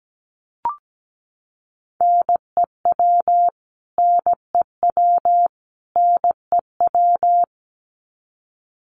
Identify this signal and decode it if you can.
Morse code, heard as a keyed tone, about 17 wpm, 700 Hz. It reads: NEW NEW NEW
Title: Top 500 Words Repeated 17wpm